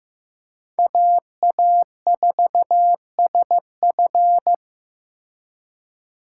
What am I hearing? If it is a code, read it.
Morse code, heard as a keyed tone, about 18 wpm, 700 Hz. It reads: AA4SF
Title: Speed Racing - USA 2x2 Call Signs 15wpm